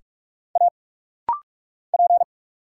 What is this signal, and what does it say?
Morse code, heard as a keyed tone, about 46 wpm, 700 Hz. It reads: A P
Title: Single Letter-Number Alt - Rapid-Fire 45wpm